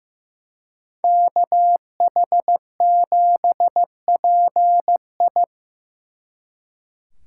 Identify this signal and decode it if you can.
Morse code, heard as a keyed tone, about 18 wpm, 700 Hz. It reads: KH7PI
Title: Speed Racing - USA 2x2 Call Signs 15wpm